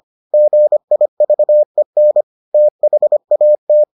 Morse code, heard as a keyed tone, about 25 wpm, 600 Hz. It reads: GIVEN THAT